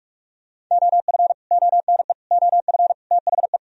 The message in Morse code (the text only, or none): OF ONE OF THE